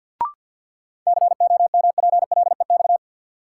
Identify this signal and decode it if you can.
Morse code, heard as a keyed tone, about 53 wpm, 700 Hz. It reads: COMPLEX
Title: Sentences from Top 900 Words - New Word Drill - 50wpm